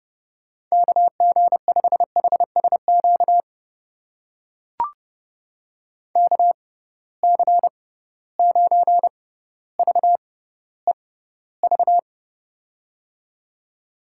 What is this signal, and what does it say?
Morse code, heard as a keyed tone, about 31 wpm, spaced as for 16 wpm, 700 Hz. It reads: KG5HSQ KC9VEV
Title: USA Call Signs - 6 Characters - 6x Character Spacing - 30wpm